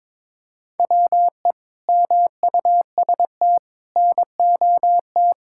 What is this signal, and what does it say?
Morse code, heard as a keyed tone, about 22 wpm, 700 Hz. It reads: WE MUST NOT
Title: Sentences from Top 1600 Words 22wpm